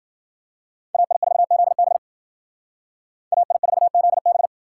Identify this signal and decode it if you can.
Morse code, heard as a keyed tone, about 52 wpm, 700 Hz. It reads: AI4ZB AI4ZB